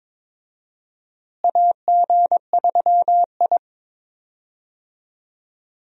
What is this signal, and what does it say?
Morse code, heard as a keyed tone, about 23 wpm, 700 Hz. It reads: AG3I
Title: USA Call Signs - 4 Characters - Repeated 2x - 22wpm